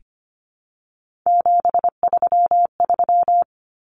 Morse code, heard as a keyed tone, about 25 wpm, 700 Hz. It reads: 733
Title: Common Cut Numbers - 3-Digit 25wpm